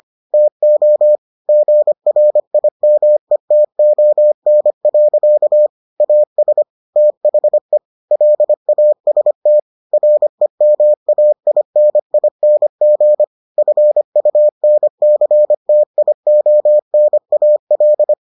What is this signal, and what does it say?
Morse code, heard as a keyed tone, about 25 wpm, 600 Hz. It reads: TO GRIMETON. AS THE LAST REMAINING FUNCTIONAL